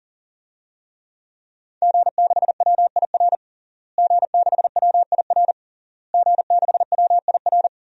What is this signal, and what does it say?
Morse code, heard as a keyed tone, about 40 wpm, 700 Hz. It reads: G6WIR G6WIR G6WIR
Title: DXCC-100 Call Signs - 5 Characters - Repeated 3x - 40wpm